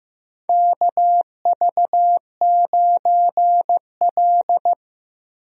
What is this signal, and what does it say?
Morse code, heard as a keyed tone, about 15 wpm, 700 Hz. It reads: KV9L
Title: USA Call Signs - 4 Characters - Repeated 2x - 15wpm